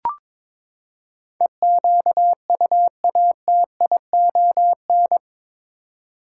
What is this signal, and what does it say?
Morse code, heard as a keyed tone, about 22 wpm, 700 Hz. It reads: EQUATION
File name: practice / 22wpm / Eight Letter Words - 22wpm